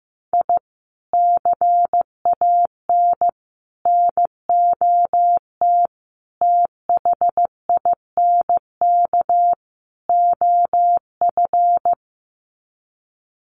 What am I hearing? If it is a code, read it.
Morse code, heard as a keyed tone, about 15 wpm, 700 Hz. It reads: I CAN NOT THINK OF
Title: Sets of 5 Words 15wpm